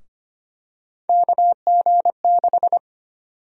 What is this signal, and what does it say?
Morse code, heard as a keyed tone, about 25 wpm, 700 Hz. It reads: KG6
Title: USA Call Sign Prefixes - 25wpm